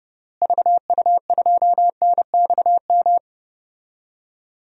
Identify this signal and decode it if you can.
Morse code, heard as a keyed tone, about 29 wpm, 700 Hz. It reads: VU2NXM
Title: DXCC-100 Call Signs - 6 Characters - Repeated 3x - 30wpm